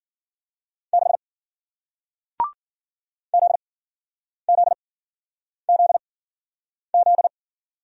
Morse code, heard as a keyed tone, about 50 wpm, spaced as for 9 wpm, 700 Hz. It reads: 6 ZZZZ
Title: Speed-Racing Single Letter-Number 40wpm